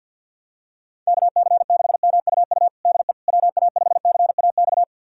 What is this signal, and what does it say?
Morse code, heard as a keyed tone, about 52 wpm, 700 Hz. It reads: KC6MUU DE WA5CAX
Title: USA Callsign DE Callsign - 6 Characters - 50wpm